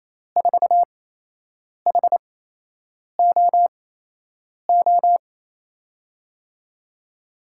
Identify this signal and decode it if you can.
Morse code, heard as a keyed tone, about 28 wpm, spaced as for 8 wpm, 700 Hz. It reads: 4HOO
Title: USA Call Signs - 6 Characters - 8x Character Spacing - 28wpm